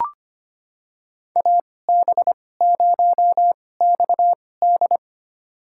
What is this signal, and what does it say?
Morse code, heard as a keyed tone, about 25 wpm, spaced as for 18 wpm, 700 Hz. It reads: AB0XD
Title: USA Call Signs - 5 Characters - 2x Character Spacing - 25wpm